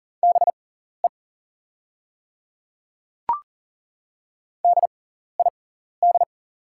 Morse code, heard as a keyed tone, about 39 wpm, spaced as for 14 wpm, 690 Hz. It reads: BE DID
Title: 6x Character Spacing - Top 100 Words - 40wpm